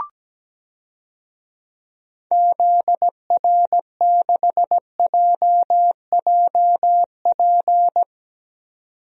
Morse code, heard as a keyed tone, about 17 wpm, 700 Hz. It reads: ZR6JJP
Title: DXCC-100 Call Signs - 6 Characters - 17wpm